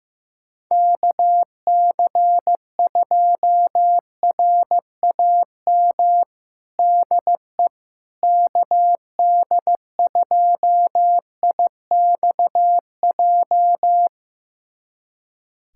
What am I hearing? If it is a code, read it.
Morse code, heard as a keyed tone, about 15 wpm, 700 Hz. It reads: KC2RAM DE KD2IXJ
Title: USA Callsign DE Callsign - 6 Characters - 15wpm